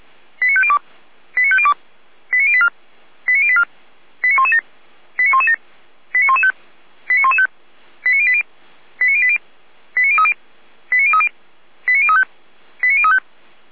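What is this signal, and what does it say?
Five-tone selective call (ZVEI), heard as quick runs of five tones, tones of about 70 ms. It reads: 89502 89502 89084 89084 89117 89117 89115 89115 89080 89080 89030 89030 89035 89035
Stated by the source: ZVEI five-tone selective call 89120 received by SDR.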